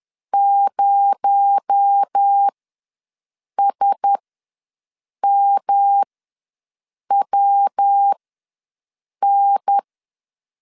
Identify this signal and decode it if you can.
Morse code, heard as a keyed tone, about 10 wpm, spaced as for 6 wpm, 800 Hz. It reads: 0SMWN
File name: LEZIONI / 7 GF9Z / 7 GF ws 10wpm 3918